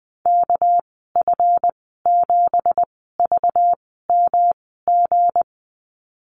Morse code, heard as a keyed tone, about 20 wpm, spaced as for 15 wpm, 700 Hz. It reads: KF7VMG
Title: 213 Lesson 34g - Number 8 - Call Signs 20wpm